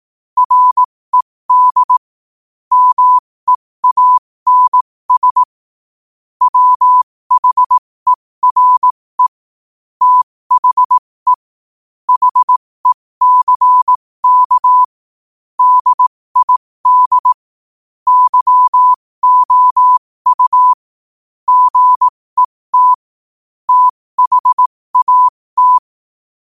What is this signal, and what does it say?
Morse code, heard as a keyed tone, about 18 wpm, spaced as for 15 wpm, 1000 Hz. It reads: RED MEANS WHERE THE HECK DID YOU GET THAT
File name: qotdfiles / 15wpm